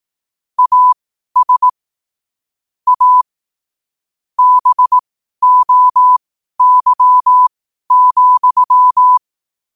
Morse code, heard as a keyed tone, about 18 wpm, spaced as for 12 wpm, 1000 Hz. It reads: AS A BOY,
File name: qotdfiles / 12wpm